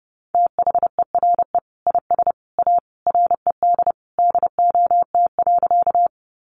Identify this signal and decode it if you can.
Morse code, heard as a keyed tone, about 30 wpm, 700 Hz. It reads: THERE IS A RED DOT.
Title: Sentences from Top 1500 Words 30wpm